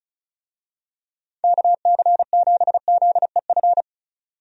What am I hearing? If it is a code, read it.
Morse code, heard as a keyed tone, about 34 wpm, 700 Hz. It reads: KC7ZEF
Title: USA Call Signs - 6 Characters - 35wpm